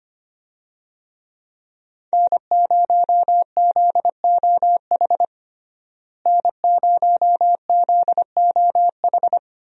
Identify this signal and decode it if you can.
Morse code, heard as a keyed tone, about 25 wpm, 700 Hz. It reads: N0ZOH N0ZOH
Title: USA Call Signs - 5 Characters - Repeated 2x - 25wpm